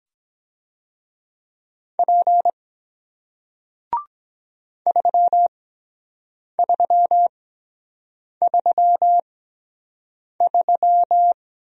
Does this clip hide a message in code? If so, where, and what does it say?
Yes: Morse code from 2.0 to 11.3 s, heard as a keyed tone, about 23 wpm, spaced as for 7 wpm, 700 Hz. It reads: P 3333